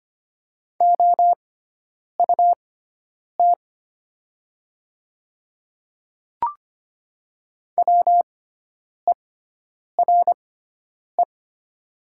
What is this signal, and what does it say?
Morse code, heard as a keyed tone, about 25 wpm, spaced as for 9 wpm, 700 Hz. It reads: OUT WERE
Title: 6x Character Spacing - Top 100 Words - 25wpm